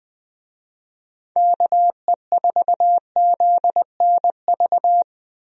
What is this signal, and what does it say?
Morse code, heard as a keyed tone, about 21 wpm, 700 Hz. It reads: KE4ZNV
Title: USA Call Signs - 6 Characters - Repeated 3x - 20wpm